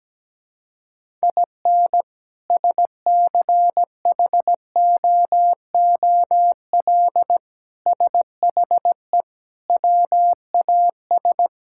Morse code, heard as a keyed tone, about 17 wpm, 700 Hz. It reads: IN SCHOOL SHE WAS
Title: Sentences from Top 1000 Words - Review Part 5 - 17wpm